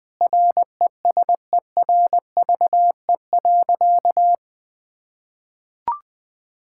Morse code, heard as a keyed tone, about 21 wpm, 700 Hz. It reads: RESERVE.
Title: Sentences from Top 1000 Words - Review Part 6 - 20wpm